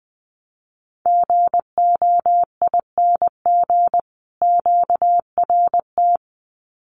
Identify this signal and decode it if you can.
Morse code, heard as a keyed tone, about 20 wpm, 700 Hz. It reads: GOING QRT
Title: Common QSO Phrases 20wpm